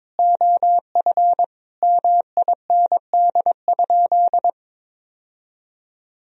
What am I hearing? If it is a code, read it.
Morse code, heard as a keyed tone, about 21 wpm, 700 Hz. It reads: OF MIND?
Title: Sentences from Top 1000 Words - Review Part 2 - 22wpm